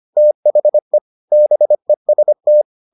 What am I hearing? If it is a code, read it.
Morse code, heard as a keyed tone, about 25 wpm, 600 Hz. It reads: THE BEST